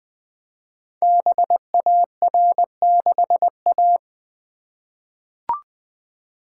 Morse code, heard as a keyed tone, about 20 wpm, 700 Hz. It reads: BAR6A
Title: Code Groups - 5 Letter:Numbers - 20wpm